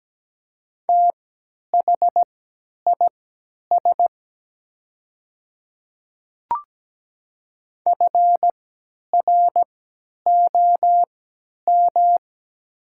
Morse code, heard as a keyed tone, about 17 wpm, spaced as for 10 wpm, 700 Hz. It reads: THIS FROM
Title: 3x Character Spacing - Top 300 Words - 17wpm